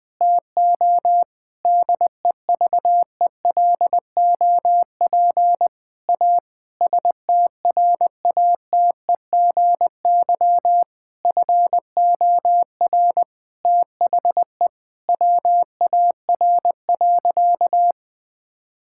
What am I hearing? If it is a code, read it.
Morse code, heard as a keyed tone, about 21 wpm, 700 Hz. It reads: TO DEVELOP A STRATEGY FOR THE WAR.
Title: Sentences from Top 1000 Words - Review Part 5 - 20wpm